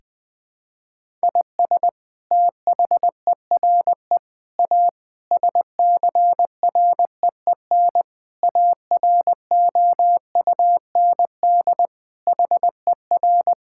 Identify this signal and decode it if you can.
Morse code, heard as a keyed tone, about 20 wpm, 700 Hz. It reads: IS THERE A SCREEN AROUND HER
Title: Sentences from Top 1000 Words - Review Part 6 - 20wpm